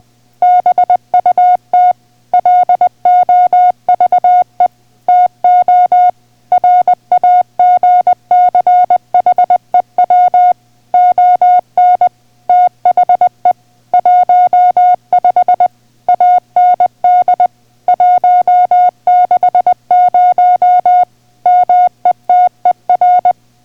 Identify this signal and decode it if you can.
Morse code, heard as a keyed tone, about 20 wpm, 720 Hz. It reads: BUT LOVE TO RAGCHEW ON THE 15 AND 160 METER